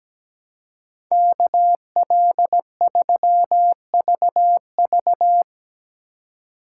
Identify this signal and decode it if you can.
Morse code, heard as a keyed tone, about 19 wpm, 700 Hz. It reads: KL3VV
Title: Speed Racing - USA 2x2 Call Signs 17wpm